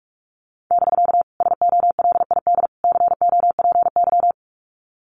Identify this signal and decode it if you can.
Morse code, heard as a keyed tone, about 44 wpm, 700 Hz. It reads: <BK> SOLID COPY
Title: Common QSO Phrases Encore 45wpm